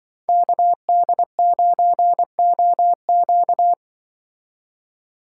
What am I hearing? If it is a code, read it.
Morse code, heard as a keyed tone, about 24 wpm, 700 Hz. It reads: KD9OQ
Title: Speed Racing - USA 2x2 Call Signs 20wpm